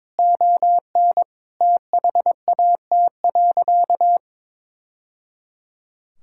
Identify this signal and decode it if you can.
Morse code, heard as a keyed tone, about 22 wpm, 700 Hz. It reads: ON THAT.
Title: Sentences from Top 1000 Words - Review Part 5 - 22wpm